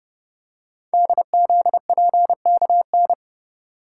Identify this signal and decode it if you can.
Morse code, heard as a keyed tone, about 29 wpm, 700 Hz. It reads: DZPKN
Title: Code Groups - 5 Letters - 30wpm